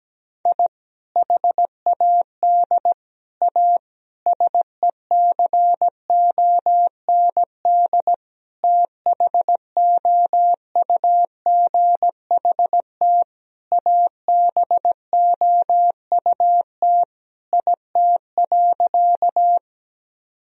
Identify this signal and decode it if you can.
Morse code, heard as a keyed tone, about 17 wpm, 700 Hz. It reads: I HAD A SECOND THOUGHT ABOUT IT.
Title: Sentences from Top 1000 Words - Review Part 1 - 17wpm